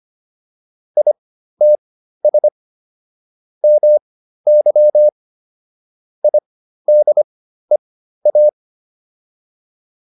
Morse code, heard as a keyed tone, about 25 wpm, spaced as for 13 wpm, 600 Hz. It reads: ITS MY IDEA